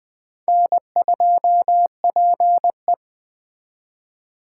Morse code, heard as a keyed tone, about 19 wpm, 700 Hz. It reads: N2PE